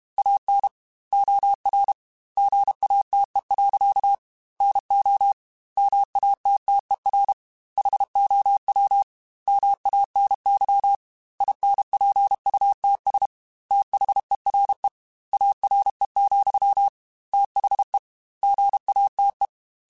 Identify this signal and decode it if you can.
Morse code, heard as a keyed tone, about 32 wpm, 800 Hz. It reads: AN OR GATE. NO MATTER HOW MANY INPUTS THERE ARE, THE GATE
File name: Technical Subjects / Technical Subjects-mp3s / OR Gate @30